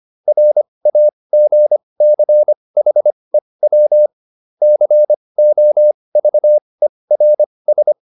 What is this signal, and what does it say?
Morse code, heard as a keyed tone, about 25 wpm, spaced as for 20 wpm, 600 Hz. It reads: RAGCHEW COVERS
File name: SS306 20